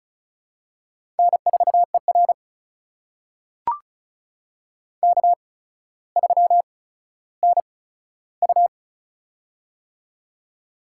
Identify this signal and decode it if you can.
Morse code, heard as a keyed tone, about 36 wpm, spaced as for 15 wpm, 700 Hz. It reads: N4ER K3NU
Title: USA Call Signs - 4 Characters - 8x Character Spacing - 35wpm